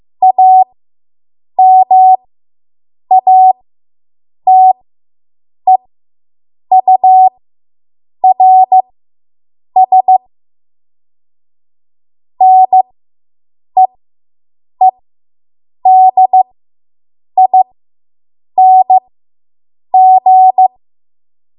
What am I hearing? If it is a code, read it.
Morse code, heard as a keyed tone, about 15 wpm, spaced as for 7 wpm, 750 Hz. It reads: AMATEURS NEEDING